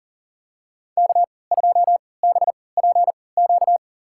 Morse code, heard as a keyed tone, about 40 wpm, spaced as for 23 wpm, 700 Hz. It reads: K2BPQ
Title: USA Call Signs - 5 Characters - 3x Character Spacing - 40wpm